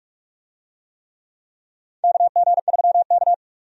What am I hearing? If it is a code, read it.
Morse code, heard as a keyed tone, about 44 wpm, 700 Hz. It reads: KG3K
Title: USA Call Signs - 4 Characters - Repeated 2x - 45wpm